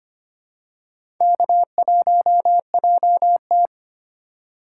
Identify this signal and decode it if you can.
Morse code, heard as a keyed tone, about 25 wpm, 700 Hz. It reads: K1JT
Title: USA Call Signs - 4 Characters - 3x Character Spacing - 25wpm